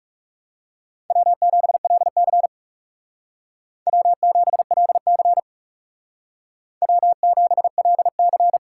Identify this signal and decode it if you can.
Morse code, heard as a keyed tone, about 36 wpm, 700 Hz. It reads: W7LC W7LC W7LC
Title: Speed Racing - USA 1x2 Call Signs 30wpm